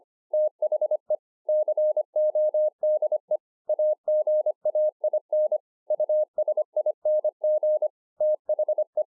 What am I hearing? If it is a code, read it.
Morse code, heard as a keyed tone, about 24 wpm, 610 Hz. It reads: THE CODE AGAIN USING THE